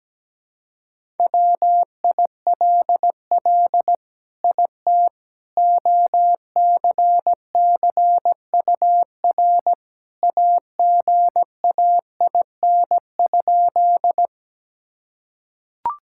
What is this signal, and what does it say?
Morse code, heard as a keyed tone, about 17 wpm, 700 Hz. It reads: WILL IT OCCUR AGAIN?
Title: Sentences from Top 1000 Words - Review Part 6 - 17wpm